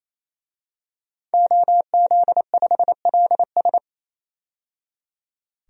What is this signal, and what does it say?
Morse code, heard as a keyed tone, about 28 wpm, 700 Hz. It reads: OZ5LS
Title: DXCC-100 Call Signs - 5 Characters - 28wpm